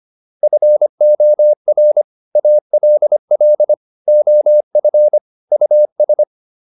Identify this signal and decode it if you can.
Morse code, heard as a keyed tone, about 25 wpm, 600 Hz. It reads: FOR ALL OF US